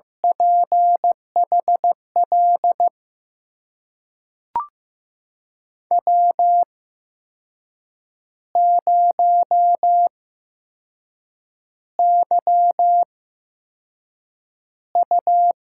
Morse code, heard as a keyed tone, about 15 wpm, spaced as for 6 wpm, 700 Hz. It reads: PHL W0YU
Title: USA Call Signs - 8x Character Spacing - 15wpm